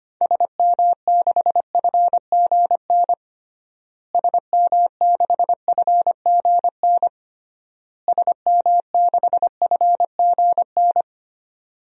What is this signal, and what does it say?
Morse code, heard as a keyed tone, about 25 wpm, 700 Hz. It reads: SM6FGN SM6FGN SM6FGN
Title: DXCC-100 Call Signs - 6 Characters - Repeated 3x - 25wpm